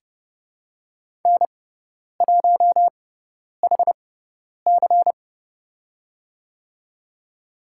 Morse code, heard as a keyed tone, about 30 wpm, spaced as for 10 wpm, 700 Hz. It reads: N1HC
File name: Lesson 16g - Letter c - Call Signs 30wpm